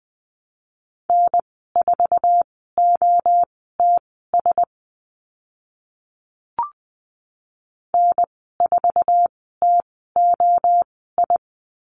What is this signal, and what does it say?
Morse code, heard as a keyed tone, about 20 wpm, spaced as for 14 wpm, 700 Hz. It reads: N4OTS N4TOI
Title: 076 Lesson 13g - Letter L - Call Signs 20wpm